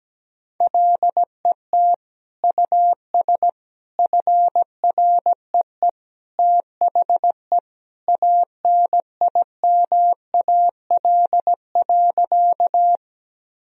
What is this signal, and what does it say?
Morse code, heard as a keyed tone, about 17 wpm, 700 Hz. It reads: LET US FREE THE ANIMAL.
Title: Sentences from Top 1000 Words - Review Part 1 - 17wpm